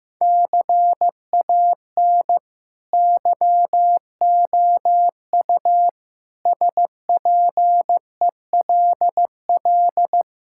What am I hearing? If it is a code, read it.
Morse code, heard as a keyed tone, about 15 wpm, 700 Hz. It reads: CAN YOU SPELL
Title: Sentences from Top 1000 Words - Review Part 5 - 15wpm